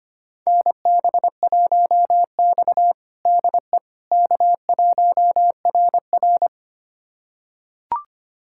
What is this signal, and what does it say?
Morse code, heard as a keyed tone, about 25 wpm, 700 Hz. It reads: NB1X DE K1RR
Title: USA Callsign DE Callsign - 4 Characters - 25wpm